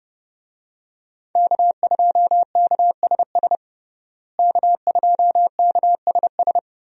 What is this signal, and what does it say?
Morse code, heard as a keyed tone, about 30 wpm, 700 Hz. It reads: K2KSS K2KSS
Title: USA Call Signs - 5 Characters - Repeated 2x - 30wpm